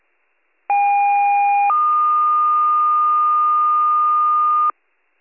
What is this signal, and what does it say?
Two-tone page, heard as a one-second tone, then a three-second tone, about 810 then 1220 Hz. It is heard from 0.7 to 4.7 s.